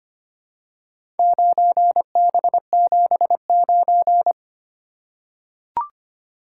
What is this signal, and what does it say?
Morse code, heard as a keyed tone, about 25 wpm, 700 Hz. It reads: 9B79
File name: practice / 25wpm / Four Letter-Number Combinations - 25wpm